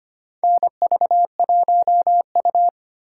Morse code, heard as a keyed tone, about 25 wpm, 700 Hz. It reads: NV1U
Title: K1USN SST QSO Practice - 25wpm